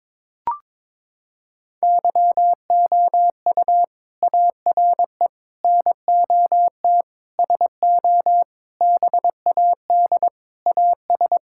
Morse code, heard as a keyed tone, about 22 wpm, 700 Hz. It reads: YOU ARE NOT SO BAD AS
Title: Sentences from Top 1000 Words - Review Part 6 - 22wpm